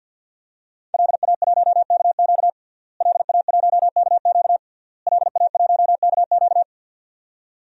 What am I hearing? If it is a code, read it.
Morse code, heard as a keyed tone, about 46 wpm, 700 Hz. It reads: LA1KQ LA1KQ LA1KQ
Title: DXCC-100 Call Signs - 5 Characters - Repeated 3x - 50wpm